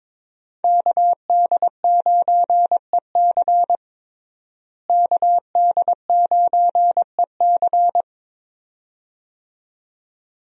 Morse code, heard as a keyed tone, about 22 wpm, 700 Hz. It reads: KD9EC KD9EC
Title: USA Call Signs - 5 Characters - Repeated 2x - 22wpm